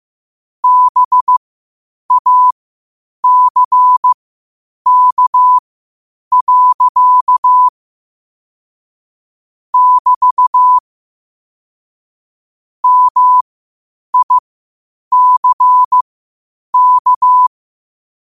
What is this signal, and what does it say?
Morse code, heard as a keyed tone, about 15 wpm, spaced as for 8 wpm, 1000 Hz. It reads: BACK. <BT> MICK
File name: 8wpm